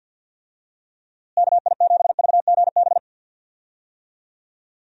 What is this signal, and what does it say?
Morse code, heard as a keyed tone, about 50 wpm, 700 Hz. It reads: KI7VGB
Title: USA Call Signs - 6 Characters - Repeated 2x - 50wpm